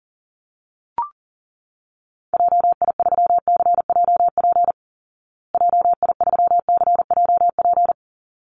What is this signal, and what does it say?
Morse code, heard as a keyed tone, about 40 wpm, 700 Hz. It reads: JI3CJP JI3CJP
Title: Call Signs Repeated 3x - 40wpm